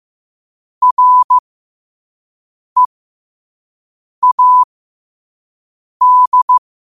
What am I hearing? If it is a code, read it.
Morse code, heard as a keyed tone, about 15 wpm, spaced as for 5 wpm, 1000 Hz. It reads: READ
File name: qotdfiles / 5wpm